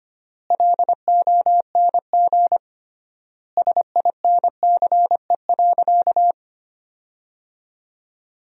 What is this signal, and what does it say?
Morse code, heard as a keyed tone, about 25 wpm, 700 Hz. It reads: LONG SINCE.
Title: 3x Word-Spacing - Sentences from Top 300 Words - 25wpm